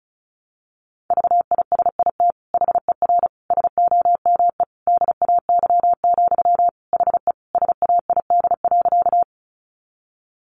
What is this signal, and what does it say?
Morse code, heard as a keyed tone, about 34 wpm, 700 Hz. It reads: VISIT HER SOME DAY, HE SAID.
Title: Sentences from Top 900 Words 35wpm